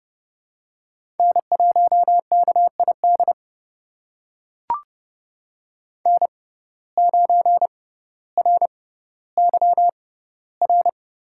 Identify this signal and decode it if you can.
Morse code, heard as a keyed tone, about 30 wpm, spaced as for 15 wpm, 700 Hz. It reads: N1KID N9RYR
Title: USA Call Signs - 6x Character Spacing - 30wpm